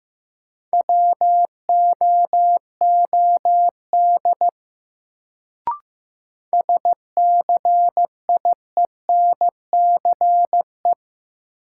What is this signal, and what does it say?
Morse code, heard as a keyed tone, about 15 wpm, 700 Hz. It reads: WOOD SCIENCE
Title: Sentences from Top 1000 Words - Review Part 2 - New Word Drill - 15wpm